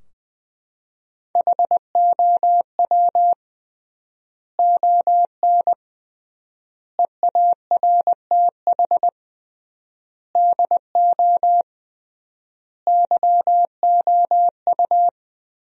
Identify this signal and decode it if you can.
Morse code, heard as a keyed tone, about 21 wpm, 700 Hz. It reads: HOW ON EARTH DO YOU
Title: 3x Word-Spacing - Sentences from Top 200 Words - 20wpm